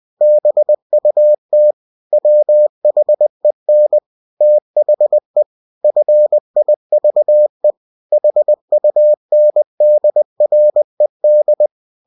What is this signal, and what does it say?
Morse code, heard as a keyed tone, about 20 wpm, 600 Hz. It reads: BUT WHEN THE FIVE HUNDRED
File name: cw-books / winnie-the-pooh / winnie-20wpm-page-0040